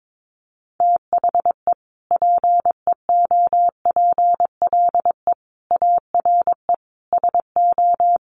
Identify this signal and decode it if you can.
Morse code, heard as a keyed tone, about 22 wpm, 700 Hz. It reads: THE PEOPLE ARE SO
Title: Additional Easy Phrases 22wpm